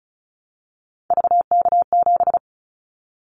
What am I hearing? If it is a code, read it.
Morse code, heard as a keyed tone, about 36 wpm, 700 Hz. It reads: VK7
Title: Callsign Prefixes for Top 100 DXCC - 35wpm